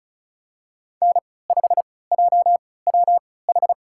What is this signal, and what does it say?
Morse code, heard as a keyed tone, about 36 wpm, spaced as for 20 wpm, 700 Hz. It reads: N5JWH
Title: USA Call Signs - 3x Character Spacing - 35wpm